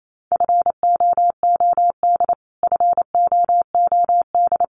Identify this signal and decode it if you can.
Morse code, heard as a keyed tone, about 28 wpm, 700 Hz. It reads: FOOD FOOD
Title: Top 500 Words Repeated 28wpm